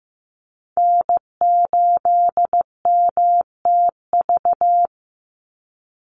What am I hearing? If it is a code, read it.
Morse code, heard as a keyed tone, about 15 wpm, 700 Hz. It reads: N8MTV
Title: Call Signs Repeated 3x - 15wpm